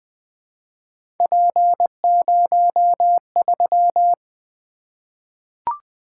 Morse code, heard as a keyed tone, about 20 wpm, 700 Hz. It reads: P03
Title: Three Letter-Number Combinations - 20wpm